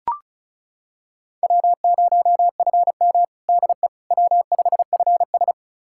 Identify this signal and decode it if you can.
Morse code, heard as a keyed tone, about 35 wpm, 700 Hz. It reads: W0FM DE W5FS
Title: USA Callsign DE Callsign - 4 Characters - 35wpm